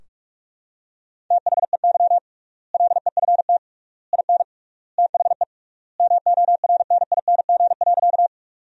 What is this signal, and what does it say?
Morse code, heard as a keyed tone, about 46 wpm, 700 Hz. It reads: THEY LEFT IN THE MORNING.
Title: 3x Word-Spacing - Sentences from Top 400 Words - 45wpm